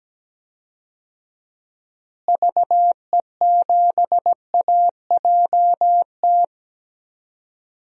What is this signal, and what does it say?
Morse code, heard as a keyed tone, about 17 wpm, 700 Hz. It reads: VE7AJT
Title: DXCC-100 Call Signs - 6 Characters - Repeated 2x - 17wpm